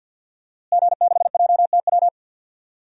Morse code, heard as a keyed tone, about 52 wpm, 700 Hz. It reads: G6JTW
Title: DXCC-100 Call Signs - 5 Characters - Repeated 3x - 50wpm